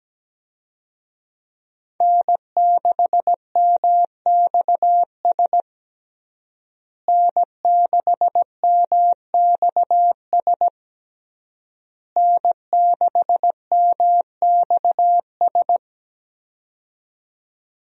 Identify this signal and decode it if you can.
Morse code, heard as a keyed tone, about 17 wpm, 700 Hz. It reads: N6MXS N6MXS N6MXS
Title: USA Call Signs - 5 Characters - Repeated 3x - 17wpm